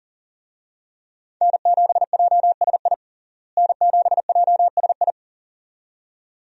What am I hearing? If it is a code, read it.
Morse code, heard as a keyed tone, about 37 wpm, 700 Hz. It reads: N7JSI N7JSI
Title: USA Call Signs - 5 Characters - Repeated 2x - 40wpm